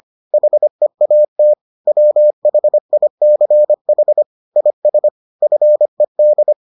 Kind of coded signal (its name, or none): Morse code